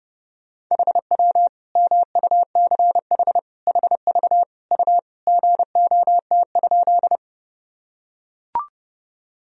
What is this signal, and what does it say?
Morse code, heard as a keyed tone, about 32 wpm, 700 Hz. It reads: HW MUCH HV U GOT?
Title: Sentences from Top 300 Words and Abbreviations 30wpm